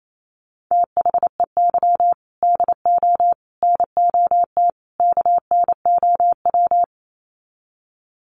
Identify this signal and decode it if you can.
Morse code, heard as a keyed tone, about 28 wpm, 700 Hz. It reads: THEY DO NOT KNOW
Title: Sets of 4 Words 28wpm